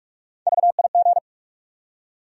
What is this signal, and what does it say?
Morse code, heard as a keyed tone, about 46 wpm, 700 Hz. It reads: VIG
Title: USA Call Sign Postfixes - 3 Letters 45wpm